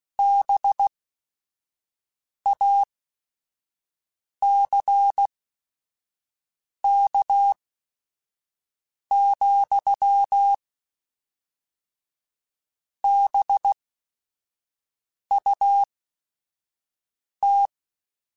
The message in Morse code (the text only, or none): BACK, BUT